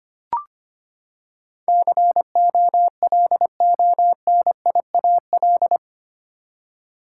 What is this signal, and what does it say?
Morse code, heard as a keyed tone, about 25 wpm, 700 Hz. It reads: COLONIAL
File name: Eight Letter Words - 25wpm